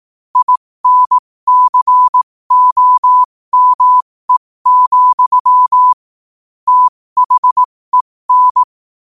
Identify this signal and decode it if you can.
Morse code, heard as a keyed tone, about 18 wpm, spaced as for 15 wpm, 1000 Hz. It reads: INCOME, THEN
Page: New quotes every day in morse code at 15 Words per minute.